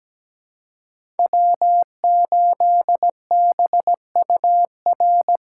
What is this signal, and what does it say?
Morse code, heard as a keyed tone, about 17 wpm, 700 Hz. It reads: W8BUR
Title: USA Call Signs - 4x Character Spacing - 17wpm